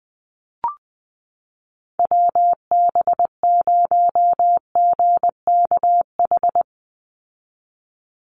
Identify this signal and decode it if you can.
Morse code, heard as a keyed tone, about 20 wpm, 700 Hz. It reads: WB0GKH
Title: Call Signs Repeated 3x - 20wpm